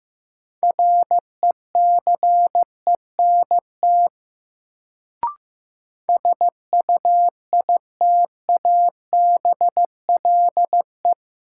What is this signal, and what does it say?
Morse code, heard as a keyed tone, about 15 wpm, 700 Hz. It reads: RECENT SUITABLE
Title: Sentences from Top 1400 Words - New Word Drill Alt - 15wpm